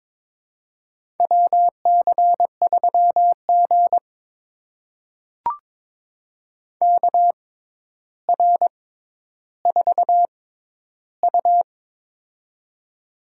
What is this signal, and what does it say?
Morse code, heard as a keyed tone, about 22 wpm, spaced as for 11 wpm, 700 Hz. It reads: WC3G KR4U
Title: USA Call Signs - 4 Characters - 6x Character Spacing - 22wpm